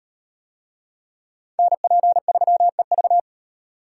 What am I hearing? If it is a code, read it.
Morse code, heard as a keyed tone, about 36 wpm, 700 Hz. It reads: NP3EV